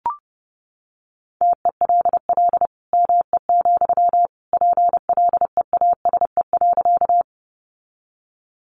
Morse code, heard as a keyed tone, about 30 wpm, 700 Hz. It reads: TELL ME, PLEASE.
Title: Sentences from Top 1500 Words 30wpm